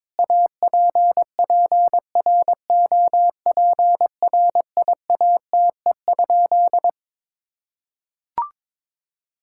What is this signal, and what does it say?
Morse code, heard as a keyed tone, about 22 wpm, 700 Hz. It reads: APPROPRIATE?
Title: Sentences from Top 1000 Words - Review Part 5 - 22wpm